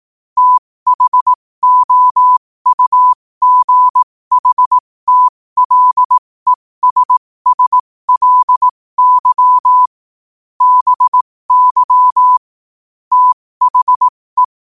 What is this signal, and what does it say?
Morse code, heard as a keyed tone, about 18 wpm, spaced as for 15 wpm, 1000 Hz. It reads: THOUGHTLESSLY BY THE
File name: qotdfiles / 15wpm